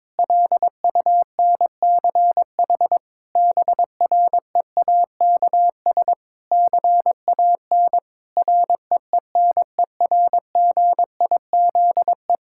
Morse code, heard as a keyed tone, about 22 wpm, 700 Hz. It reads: LUNCH BREAKS CAN REENERGIZE
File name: Difficult Sentences with Hints - 22wpm